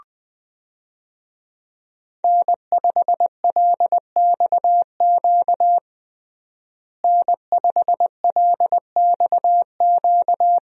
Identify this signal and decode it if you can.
Morse code, heard as a keyed tone, about 20 wpm, 700 Hz. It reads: N5LXQ N5LXQ
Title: USA Call Signs - 5 Characters - Repeated 2x - 20wpm